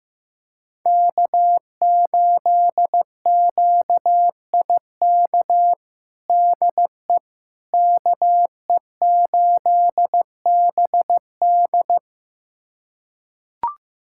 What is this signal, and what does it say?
Morse code, heard as a keyed tone, about 15 wpm, 700 Hz. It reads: K8QIK DE KE8BD
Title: USA Callsign DE Callsign - 5 Characters - 15wpm